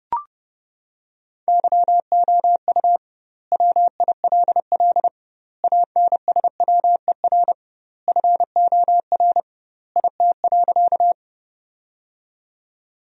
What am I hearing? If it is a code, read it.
Morse code, heard as a keyed tone, about 32 wpm, 700 Hz. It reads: YOU WILL ANSWER FOR IT.
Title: 2x Word-Spacing - Sentences from Top 300 Words - 30wpm